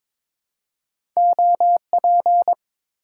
Morse code, heard as a keyed tone, about 22 wpm, 700 Hz. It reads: OP
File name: Top 90 QSO Elements - 4x Character Spacing - 22wpm